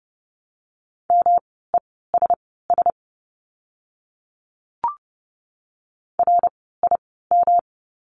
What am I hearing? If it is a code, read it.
Morse code, heard as a keyed tone, about 30 wpm, spaced as for 17 wpm, 700 Hz. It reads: MESS RIM
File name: course / 30:10wpm / 108 Lesson 18f - Letter m - Words 30wpm